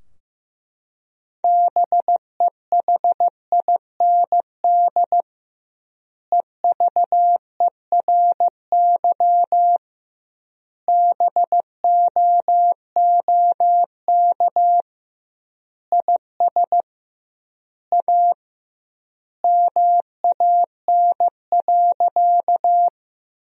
Morse code, heard as a keyed tone, about 15 wpm, 700 Hz. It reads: BEHIND EVERY BOOK IS A MAN.
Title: 2x Word-Spacing - Sentences from Top 500 Words - 15wpm